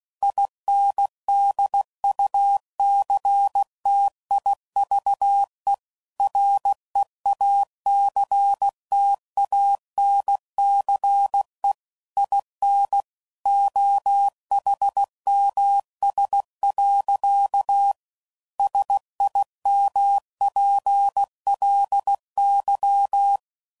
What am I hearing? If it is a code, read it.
Morse code, heard as a keyed tone, about 16 wpm, 800 Hz. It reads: INDUCTIVE REACTANCE IN OHMS. SIMPLY
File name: CW Code Groups @15WPM/Technical Subjects